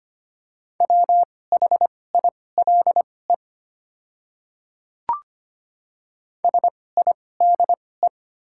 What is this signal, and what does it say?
Morse code, heard as a keyed tone, about 25 wpm, spaced as for 18 wpm, 700 Hz. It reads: WHILE SIDE
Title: 2x Character Spacing - Top 300 Words - 25wpm